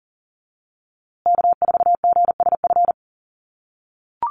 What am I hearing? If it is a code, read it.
Morse code, heard as a keyed tone, about 40 wpm, 700 Hz. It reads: K4GSF
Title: USA Call Signs Repeated 2x 40wpm